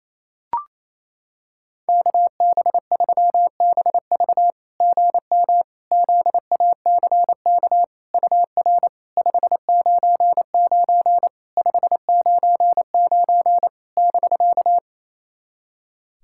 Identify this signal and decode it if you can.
Morse code, heard as a keyed tone, about 28 wpm, 700 Hz. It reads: KB3BV GM ZACK UR 599 599 <BK>
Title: SOTA QSO Practice 28wpm